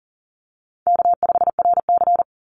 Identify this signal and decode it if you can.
Morse code, heard as a keyed tone, about 40 wpm, 700 Hz. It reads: K5RC
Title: Easy Call Signs Repeated 2x - 40wpm